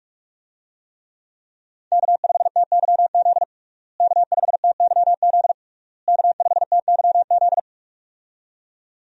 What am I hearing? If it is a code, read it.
Morse code, heard as a keyed tone, about 41 wpm, 700 Hz. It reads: K5TYZ K5TYZ K5TYZ
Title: USA Call Signs - 5 Characters - Repeated 3x - 45wpm